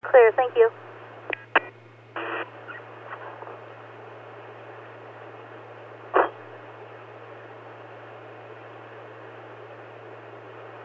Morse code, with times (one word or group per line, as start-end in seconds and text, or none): none